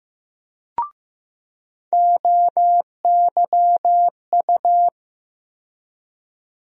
Morse code, heard as a keyed tone, about 15 wpm, 700 Hz. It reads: OYU
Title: USA Call Sign Postfixes - 3 Letters 15wpm